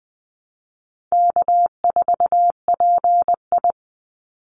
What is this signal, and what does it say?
Morse code, heard as a keyed tone, about 19 wpm, 700 Hz. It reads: K4PI
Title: Easy Call Signs Repeated 2x - 20wpm